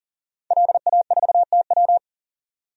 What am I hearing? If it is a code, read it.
Morse code, heard as a keyed tone, about 40 wpm, 700 Hz. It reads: LA4TW
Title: DXCC-100 Call Signs - 5 Characters - 40wpm